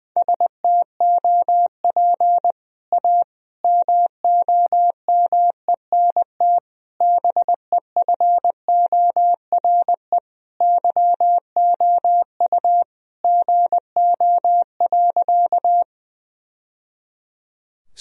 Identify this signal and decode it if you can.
Morse code, heard as a keyed tone, about 20 wpm, 700 Hz. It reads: STOP A MOMENT BEFORE YOU GO.
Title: Sentences from Top 1000 Words - Review Part 1 - 20wpm